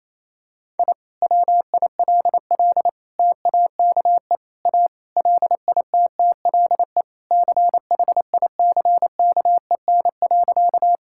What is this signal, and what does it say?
Morse code, heard as a keyed tone, about 28 wpm, 700 Hz. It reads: I WILL TAKE A LITTLE CHICKEN.
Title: Sentences from Top 1000 Words - Review Part 5 - 28wpm